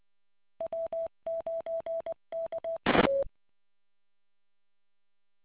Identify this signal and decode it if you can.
Morse code, heard as a keyed tone, about 20 wpm, 660 Hz. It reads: W9KE